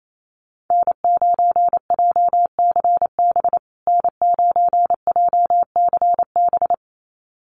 Morse code, heard as a keyed tone, about 28 wpm, 700 Hz. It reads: N9JCB N9JCB
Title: USA Call Signs Repeated 2x 28wpm